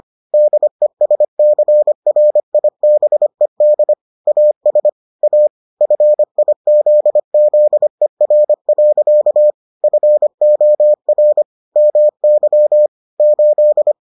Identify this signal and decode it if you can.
Morse code, heard as a keyed tone, about 25 wpm, 600 Hz. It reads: DESCRIBED AS A FIZZER. FOR MY 8